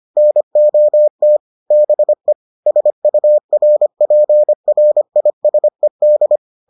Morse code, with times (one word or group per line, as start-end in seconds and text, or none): 0.2-1.4 NOT
1.7-2.3 BE
2.7-6.4 SURPRISED